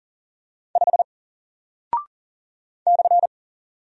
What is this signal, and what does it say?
Morse code, heard as a keyed tone, about 40 wpm, 700 Hz. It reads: 5 /
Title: Single Character (Letter:Number:Punctuation:Prosign) Alt 40wpm